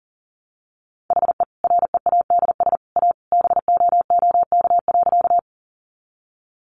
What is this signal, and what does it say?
Morse code, heard as a keyed tone, about 40 wpm, 700 Hz. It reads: HE READS A BOOK.
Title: Additional Easy Phrases 40wpm